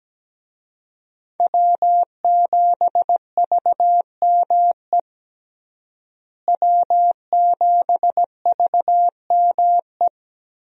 Morse code, heard as a keyed tone, about 17 wpm, 700 Hz. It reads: W7VME W7VME
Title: USA Call Signs - 5 Characters - Repeated 3x - 17wpm